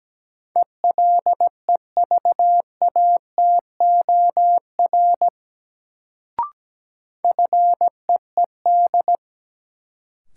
Morse code, heard as a keyed tone, about 17 wpm, 700 Hz. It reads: ELEVATOR FEED
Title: Sentences from Top 1600 Words - New Word Drill - 17wpm